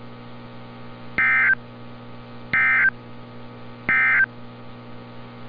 Emergency Alert System (SAME) tones, heard as harsh buzzing bursts. It is heard from 1.1 to 4.3 s.
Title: NOAA Weather Radio - Altoona, PA